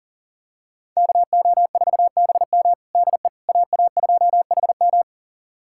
Morse code, heard as a keyed tone, about 40 wpm, 700 Hz. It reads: KO4BM DE AA2HM
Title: USA Callsign DE Callsign - 5 Characters - 40wpm